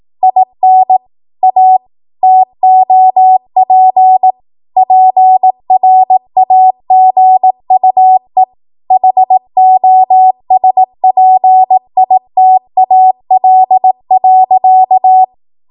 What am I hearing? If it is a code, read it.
Morse code, heard as a keyed tone, about 18 wpm, 750 Hz. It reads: IN A TOP PRAGUE HOSPITAL.
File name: Morse / 18WPM